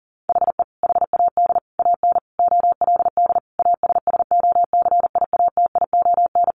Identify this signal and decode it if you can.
Morse code, heard as a keyed tone, about 40 wpm, 700 Hz. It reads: HE HAD AN OLD ASSOCIATION
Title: Sentences from Top 1100 Words 40wpm